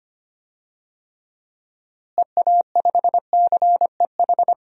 Morse code, heard as a keyed tone, about 25 wpm, 700 Hz. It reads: EA5CEH
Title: DXCC-100 Call Signs - 6 Characters - Repeated 2x - 25wpm